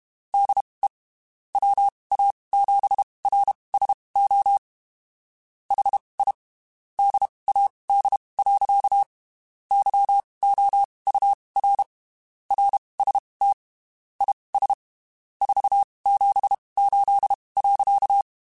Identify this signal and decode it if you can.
Morse code, heard as a keyed tone, about 32 wpm, spaced as for 23 wpm, 800 Hz. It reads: DE WA7RSO HI DAD. YOUR RST IS 478.
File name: (spaced) / QSOs Spaced-mp3s / WD0FBT @30 sp